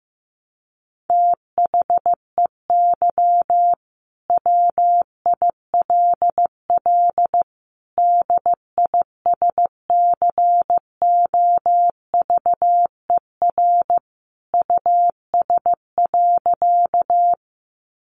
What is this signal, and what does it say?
Morse code, heard as a keyed tone, about 15 wpm, 700 Hz. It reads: THEY WILL DISCOVER US.
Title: Sentences from Top 1100 Words 15wpm